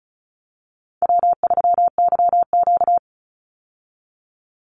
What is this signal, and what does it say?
Morse code, heard as a keyed tone, about 36 wpm, 700 Hz. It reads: W3YQ
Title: Call Signs - 35wpm